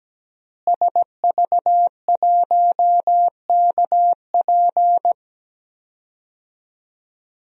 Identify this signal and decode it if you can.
Morse code, heard as a keyed tone, about 17 wpm, 700 Hz. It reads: SV1KP
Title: DXCC-100 Call Signs - 5 Characters - Repeated 2x - 17wpm